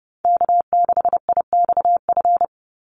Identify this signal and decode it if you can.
Morse code, heard as a keyed tone, about 30 wpm, 700 Hz. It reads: K6IXF
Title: USA Call Signs 30wpm